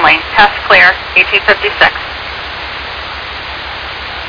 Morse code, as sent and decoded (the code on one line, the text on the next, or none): none